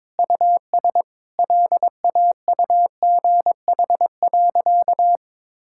Morse code, heard as a keyed tone, about 22 wpm, 700 Hz. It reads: US LAUGH.
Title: Sentences from Top 1000 Words - Review Part 5 - 22wpm